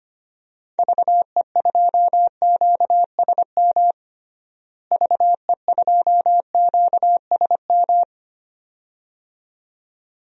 Morse code, heard as a keyed tone, about 25 wpm, 700 Hz. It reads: VE2QSM VE2QSM
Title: DXCC-100 Call Signs - 6 Characters - Repeated 2x - 25wpm